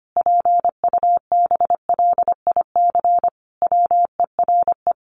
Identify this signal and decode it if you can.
Morse code, heard as a keyed tone, about 25 wpm, 700 Hz. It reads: PUBLIC WERE
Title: Sentences from Top 800 Words 25wpm